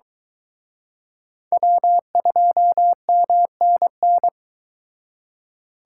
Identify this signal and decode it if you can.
Morse code, heard as a keyed tone, about 23 wpm, 700 Hz. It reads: W2MNN